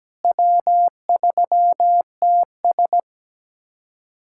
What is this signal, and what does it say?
Morse code, heard as a keyed tone, about 17 wpm, 700 Hz. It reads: W3TS
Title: USA Callsign DE Callsign - 4 Characters - 17wpm